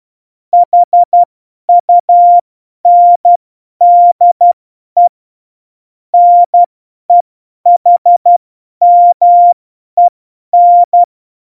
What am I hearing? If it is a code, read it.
Morse code, heard as a keyed tone, about 12 wpm, spaced as for 10 wpm, 700 Hz. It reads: HUNDE NEHMEN
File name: Morsekurs / Witze / Witz 20. November 2018, 50 F70